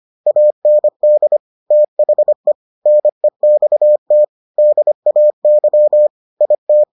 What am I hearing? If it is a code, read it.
Morse code, heard as a keyed tone, about 25 wpm, 600 Hz. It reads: AND THE NEXT DAY IT